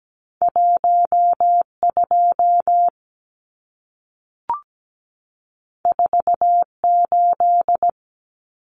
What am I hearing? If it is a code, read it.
Morse code, heard as a keyed tone, about 17 wpm, 700 Hz. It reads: 12 48
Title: Numbers - 2-Digit 17wpm